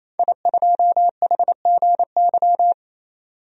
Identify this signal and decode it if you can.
Morse code, heard as a keyed tone, about 28 wpm, 700 Hz. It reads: I2HGY
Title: DXCC-100 Call Signs - 5 Characters - 28wpm